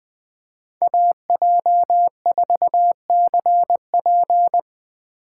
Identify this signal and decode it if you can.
Morse code, heard as a keyed tone, about 21 wpm, 700 Hz. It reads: AJ4CP
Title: USA Call Signs - 5 Characters - 2x Character Spacing - 20wpm